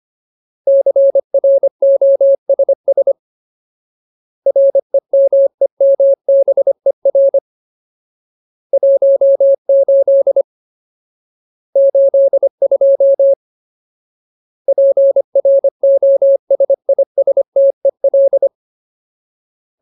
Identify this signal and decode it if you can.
Morse code, heard as a keyed tone, about 25 wpm, 550 Hz. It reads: CROSS REMEMBER 18 82 PROSISTEL
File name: Ultimate Word List / 25WPM Ultimate